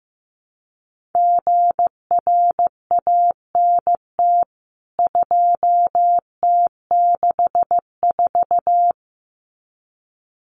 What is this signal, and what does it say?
Morse code, heard as a keyed tone, about 15 wpm, 700 Hz. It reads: GRANT 2T64
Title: CWT Exchange 15wpm